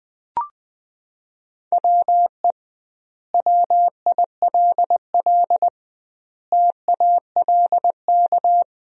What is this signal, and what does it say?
Morse code, heard as a keyed tone, about 20 wpm, 700 Hz. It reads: WE WILL TALK
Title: 2x Word-Spacing - Sentences from Top 400 Words - 20wpm